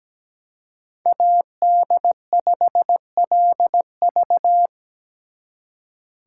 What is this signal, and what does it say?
Morse code, heard as a keyed tone, about 20 wpm, 700 Hz. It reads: AD5LV
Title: Speed Racing - USA Call Signs 17wpm